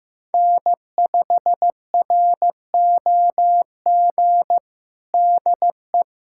Morse code, heard as a keyed tone, about 15 wpm, 700 Hz. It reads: N5ROG DE
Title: USA Callsign DE Callsign - 5 Characters - 15wpm